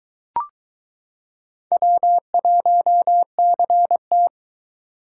Morse code, heard as a keyed tone, about 22 wpm, 700 Hz. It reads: W1CT